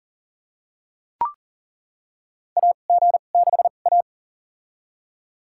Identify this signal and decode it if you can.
Morse code, heard as a keyed tone, about 38 wpm, spaced as for 28 wpm, 690 Hz. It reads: AG6A
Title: USA Call Signs - 4 Characters - 2x Character Spacing - 40wpm